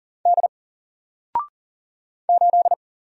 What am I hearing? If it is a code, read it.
Morse code, heard as a keyed tone, about 40 wpm, 700 Hz. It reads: D 8
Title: Single Letter-Number 40wpm